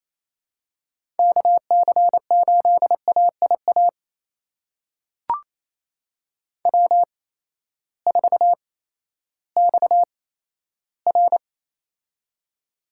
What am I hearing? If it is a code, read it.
Morse code, heard as a keyed tone, about 28 wpm, spaced as for 14 wpm, 700 Hz. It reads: KC8AIA W4XR
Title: USA Call Signs - 8x Character Spacing - 28wpm